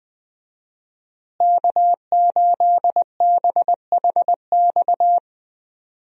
Morse code, heard as a keyed tone, about 20 wpm, 700 Hz. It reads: K8BHX
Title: USA Call Signs - 5 Characters - Repeated 3x - 20wpm